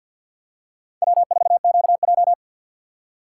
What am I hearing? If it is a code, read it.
Morse code, heard as a keyed tone, about 51 wpm, 700 Hz. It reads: W4QJ
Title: USA Call Signs - 2x Character Spacing - 50wpm